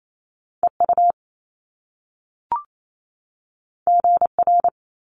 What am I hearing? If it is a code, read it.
Morse code, heard as a keyed tone, about 28 wpm, 700 Hz. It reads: EU GR
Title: Top Two Letter Combinations in Words 28wpm